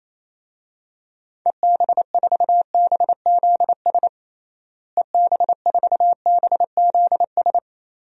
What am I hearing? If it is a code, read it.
Morse code, heard as a keyed tone, about 28 wpm, 700 Hz. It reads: EB4BZS EB4BZS
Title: DXCC-100 Call Signs - 6 Characters - Repeated 3x - 28wpm